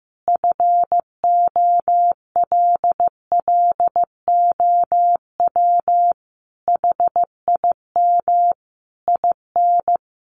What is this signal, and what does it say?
Morse code, heard as a keyed tone, about 15 wpm, 700 Hz. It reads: FOLLOW HIM IN
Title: Sentences from Top 700 Words 15wpm